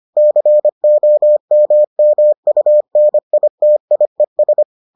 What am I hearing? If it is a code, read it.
Morse code, heard as a keyed tone, about 25 wpm, 600 Hz. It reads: COMMUNITIES